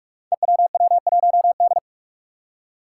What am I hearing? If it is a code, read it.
Morse code, heard as a keyed tone, about 43 wpm, 700 Hz. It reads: EWW1D
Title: Five Challenging Letter-Number Combinations 45wpm